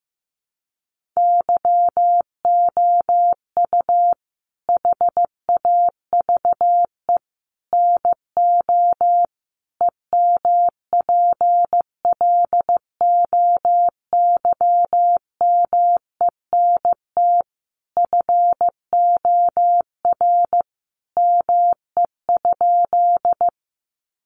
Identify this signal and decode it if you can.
Morse code, heard as a keyed tone, about 15 wpm, 700 Hz. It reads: YOU HAVE NO EMPLOYMENT FOR ME?
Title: Sentences from Top 1500 Words 15wpm